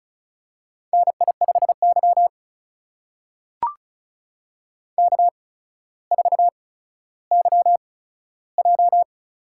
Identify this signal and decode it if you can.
Morse code, heard as a keyed tone, about 36 wpm, spaced as for 15 wpm, 700 Hz. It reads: NI5Y K4YJ
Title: USA Call Signs - 4 Characters - 8x Character Spacing - 35wpm